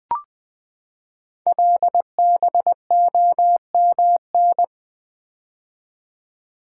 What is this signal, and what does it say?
Morse code, heard as a keyed tone, about 21 wpm, 700 Hz. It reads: LBOMN
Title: Code Groups - 5 Letters - 20wpm